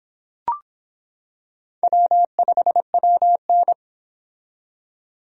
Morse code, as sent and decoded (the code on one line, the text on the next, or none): .-- ..... .-- -.
W5WN